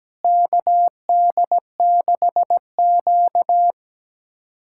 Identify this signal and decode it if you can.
Morse code, heard as a keyed tone, about 17 wpm, 700 Hz. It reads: KD6Q
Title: USA Callsign DE Callsign - 4 Characters - 17wpm